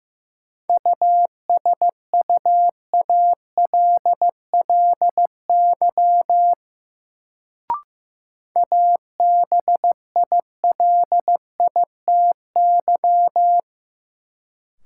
Morse code, heard as a keyed tone, about 15 wpm, 700 Hz. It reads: USUALLY ABILITY
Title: Sentences from Top 1000 Words - Review Part 3 - New Word Drill - 15wpm